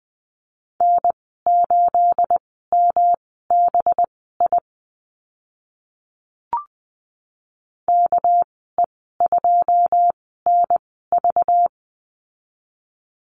Morse code, heard as a keyed tone, about 20 wpm, spaced as for 15 wpm, 700 Hz. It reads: N8MBI KE2NV
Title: 231 Lesson 37g - Letter x - Call Signs 20wpm